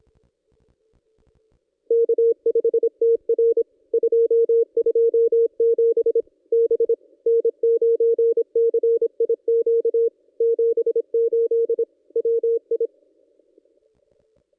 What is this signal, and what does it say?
Morse code, heard as a keyed tone, about 25 wpm, 460 Hz. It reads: K5TR 227 B N9CIQ 78 WI
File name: contests / ss-cw-k5tr-op-n5rz -1600